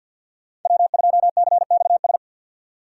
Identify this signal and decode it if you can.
Morse code, heard as a keyed tone, about 52 wpm, 700 Hz. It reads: W2CXS
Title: Code Groups - 5 Letter:Numbers - 50wpm